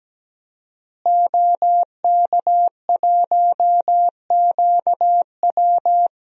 Morse code, heard as a keyed tone, about 17 wpm, 700 Hz. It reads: OK1QW
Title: DXCC-100 Call Signs - 5 Characters - Repeated 2x - 17wpm